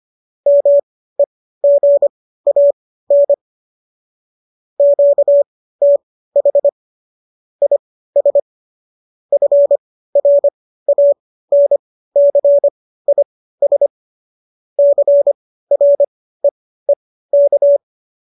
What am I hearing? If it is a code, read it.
Morse code, heard as a keyed tone, about 25 wpm, spaced as for 15 wpm, 570 Hz. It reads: MEGAN QTH IS FRANCIS CREEK